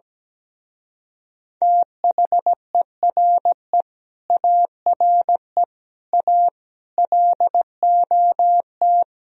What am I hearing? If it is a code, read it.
Morse code, heard as a keyed tone, about 17 wpm, 700 Hz. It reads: THERE ARE A LOT
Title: Speed-Racing - Sets of 5 Words 17wpm